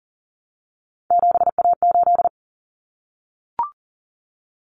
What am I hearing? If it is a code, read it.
Morse code, heard as a keyed tone, about 40 wpm, 700 Hz. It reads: 7A8
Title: Common Cut Numbers - 3-Digit 40wpm